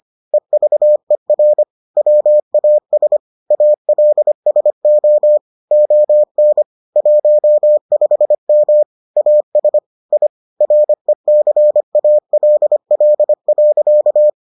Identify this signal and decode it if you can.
Morse code, heard as a keyed tone, about 25 wpm, 600 Hz. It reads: EVER WAS ALSO ON 15M AS I RECALL.